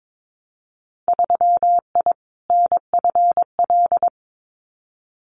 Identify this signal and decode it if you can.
Morse code, heard as a keyed tone, about 22 wpm, 700 Hz. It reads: 3I NFL
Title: Winter Field Day Exchanges 22wpm